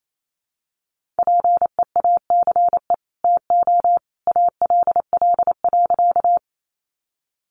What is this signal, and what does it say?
Morse code, heard as a keyed tone, about 28 wpm, 700 Hz. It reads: PEACE TO ALL.
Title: Sentences from Top 1500 Words 28wpm